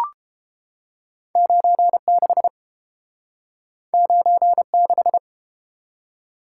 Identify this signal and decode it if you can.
Morse code, heard as a keyed tone, about 32 wpm, 700 Hz. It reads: 96 96